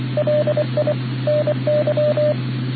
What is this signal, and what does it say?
Morse code, heard as a keyed tone, about 25 wpm, 600 Hz, with radio noise. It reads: LI NY